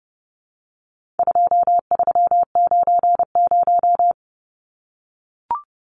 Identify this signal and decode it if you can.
Morse code, heard as a keyed tone, about 30 wpm, 700 Hz. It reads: 2390
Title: Numbers - 4-Digit 30wpm